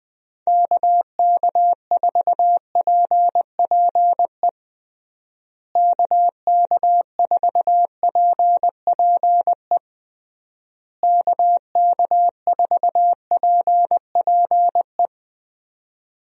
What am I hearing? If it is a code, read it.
Morse code, heard as a keyed tone, about 20 wpm, 700 Hz. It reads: KK4PPE KK4PPE KK4PPE
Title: USA Call Signs - 6 Characters - Repeated 3x - 20wpm